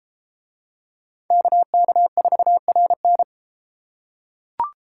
Morse code, heard as a keyed tone, about 32 wpm, 700 Hz. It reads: KK4RN